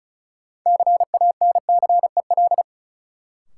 Morse code, heard as a keyed tone, about 34 wpm, 700 Hz. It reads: CANCEL
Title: Sentences from Top 1200 Words - New Word Drill - 35wpm